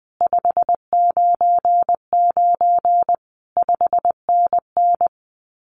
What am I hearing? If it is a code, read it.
Morse code, heard as a keyed tone, about 20 wpm, 700 Hz. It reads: 599 5NN
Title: Common QSO Phrases Encore 20wpm